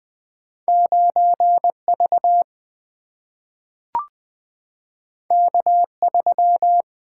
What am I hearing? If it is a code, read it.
Morse code, heard as a keyed tone, about 20 wpm, 700 Hz. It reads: 9V K3
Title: Single-Letter Single-Number 20wpm